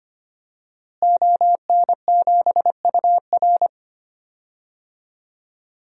Morse code, heard as a keyed tone, about 25 wpm, 700 Hz. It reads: ON7UR
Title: DXCC-100 Call Signs - 5 Characters - Repeated 3x - 25wpm